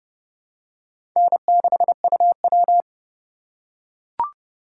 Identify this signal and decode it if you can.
Morse code, heard as a keyed tone, about 30 wpm, 700 Hz. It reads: N6UW
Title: USA Call Signs - 4 Characters - 2x Character Spacing - 30wpm